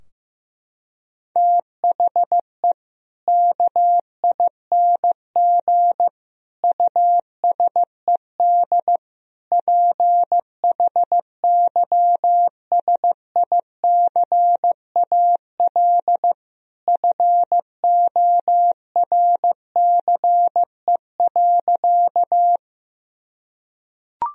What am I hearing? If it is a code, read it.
Morse code, heard as a keyed tone, about 16 wpm, 700 Hz. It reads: THE KING USED PHYSICAL FORCE.
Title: Sentences from Top 1000 Words - Review Part 4 - 15wpm